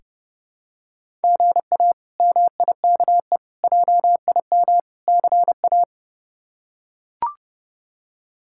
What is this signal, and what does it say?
Morse code, heard as a keyed tone, about 30 wpm, 700 Hz. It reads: GA MIKE JIM CA
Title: K1USN SST QSO Practice - 30wpm